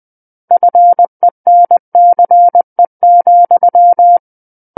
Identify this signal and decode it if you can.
Morse code, heard as a keyed tone, about 20 wpm, 700 Hz. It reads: FENCE,
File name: 20 WPM